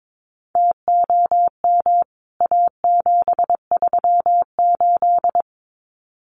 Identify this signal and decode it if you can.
Morse code, heard as a keyed tone, about 22 wpm, 700 Hz. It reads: TOM A738
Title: CWT Exchange 22wpm